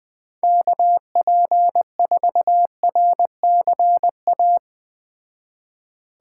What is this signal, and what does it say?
Morse code, heard as a keyed tone, about 20 wpm, 700 Hz. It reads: KP4RCA